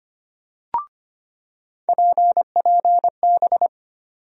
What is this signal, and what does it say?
Morse code, heard as a keyed tone, about 25 wpm, 700 Hz. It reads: PPB
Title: USA Call Sign Postfixes - 3 Letters 25wpm